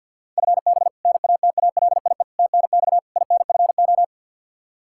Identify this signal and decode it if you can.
Morse code, heard as a keyed tone, about 52 wpm, 700 Hz. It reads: FB NATALIE TNX INFO
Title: Practice QSO - Exchange 3 - More Info - 50wpm